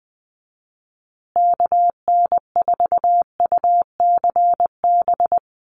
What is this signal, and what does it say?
Morse code, heard as a keyed tone, about 20 wpm, 700 Hz. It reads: KN4UCB
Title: USA Call Signs Repeated 2x 20wpm